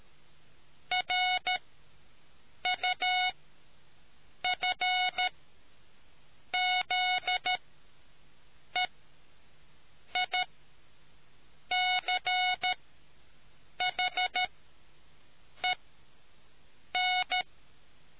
Morse code, heard as a keyed tone, about 13 wpm, spaced as for 6 wpm, 730 Hz. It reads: RUFZEICHEN